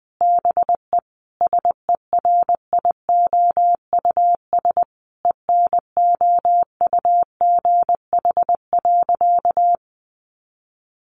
Morse code, heard as a keyed tone, about 20 wpm, 700 Hz. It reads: BE SERIOUS ENOUGH.
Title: Sentences from Top 1100 Words 20wpm